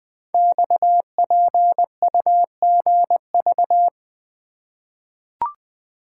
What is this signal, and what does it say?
Morse code, heard as a keyed tone, about 20 wpm, 700 Hz. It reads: XPUGV
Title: Code Groups - 5 Letters - 20wpm